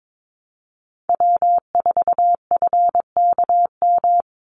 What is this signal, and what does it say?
Morse code, heard as a keyed tone, about 22 wpm, 700 Hz. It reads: W4FKM
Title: Call Signs Repeated 3x - 22wpm